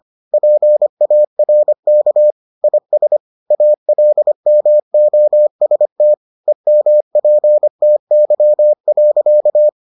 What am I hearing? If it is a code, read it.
Morse code, heard as a keyed tone, about 25 wpm, 600 Hz. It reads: PARK IS ALMOST EMPTY.